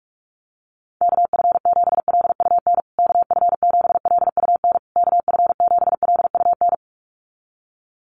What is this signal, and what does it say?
Morse code, heard as a keyed tone, about 46 wpm, 700 Hz. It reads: KF7LUN KF7LUN KF7LUN
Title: USA Call Signs Repeated 3x 45wpm